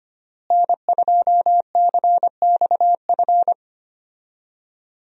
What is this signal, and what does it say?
Morse code, heard as a keyed tone, about 25 wpm, 700 Hz. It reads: N2CXF
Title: USA Call Signs - 5 Characters - Repeated 3x - 25wpm